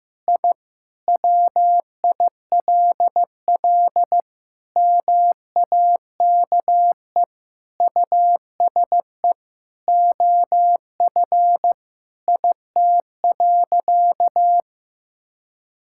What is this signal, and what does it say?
Morse code, heard as a keyed tone, about 15 wpm, 700 Hz. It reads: I WILL MAKE USE OF IT.
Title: Sentences from Top 1000 Words - Review Part 6 - 15wpm